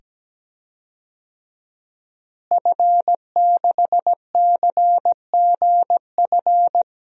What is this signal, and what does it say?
Morse code, heard as a keyed tone, about 17 wpm, 700 Hz. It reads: F6CGF
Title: DXCC-100 Call Signs - 5 Characters - Repeated 2x - 17wpm